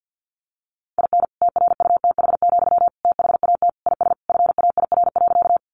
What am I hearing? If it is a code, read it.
Morse code, heard as a keyed tone, about 50 wpm, 700 Hz. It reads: IN TRUTH, THAT IS FAIR.
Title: Sentences from Top 1300 Words 50wpm